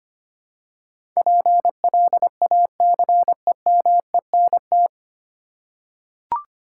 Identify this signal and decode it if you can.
Morse code, heard as a keyed tone, about 25 wpm, 700 Hz. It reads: PLACEMENT
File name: Nine Letter Words - 25wpm